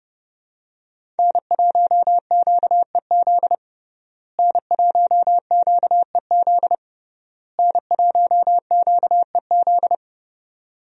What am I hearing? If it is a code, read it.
Morse code, heard as a keyed tone, about 30 wpm, 700 Hz. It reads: N1QEZ N1QEZ N1QEZ
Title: USA Call Signs - 5 Characters - Repeated 3x - 30wpm